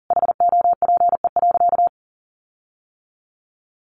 Morse code, heard as a keyed tone, about 40 wpm, 700 Hz. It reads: HOPE.
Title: Sentences from Top 1400 Words 40wpm